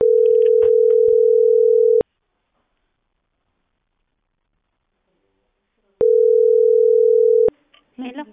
Ringback tone (North America), 440 + 480 Hz.